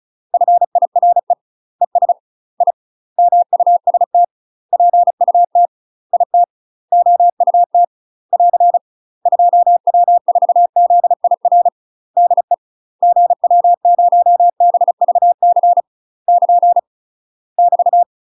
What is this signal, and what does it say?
Morse code, heard as a keyed tone, about 36 wpm, 700 Hz. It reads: FIRE ES I MUST PUT IT OUT <AR> 2W4ZIR DE GW0BVC <KN> <BT>
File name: UK MASTER QSOs TRAINING / 35WPM / UK MASTER QSOs 0032